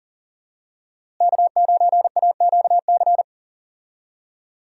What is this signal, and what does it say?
Morse code, heard as a keyed tone, about 40 wpm, 700 Hz. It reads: K9AQC
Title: USA Call Signs - 5 Characters - Repeated 2x - 40wpm